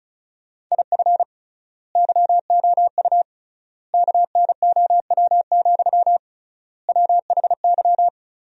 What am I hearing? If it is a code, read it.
Morse code, heard as a keyed tone, about 36 wpm, 700 Hz. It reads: IF YOU KNOW, WHY
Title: 3x Word-Spacing - Sentences from Top 300 Words - 35wpm